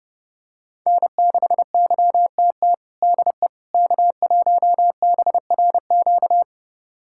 Morse code, heard as a keyed tone, about 30 wpm, 700 Hz. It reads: N6YTT DE K1BRQ
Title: USA Callsign DE Callsign 30wpm